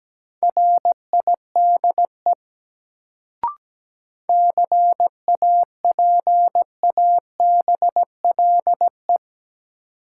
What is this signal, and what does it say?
Morse code, heard as a keyed tone, about 17 wpm, 700 Hz. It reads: RIDE CAPABLE
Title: Sentences from Top 1200 Words - New Word Drill Alt - 17wpm